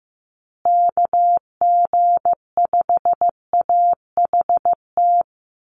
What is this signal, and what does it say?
Morse code, heard as a keyed tone, about 15 wpm, 700 Hz. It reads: KG5AHT
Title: USA Call Signs Repeated 3x 15wpm